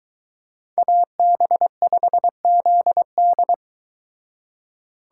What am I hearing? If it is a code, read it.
Morse code, heard as a keyed tone, about 23 wpm, 700 Hz. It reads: AB5ZD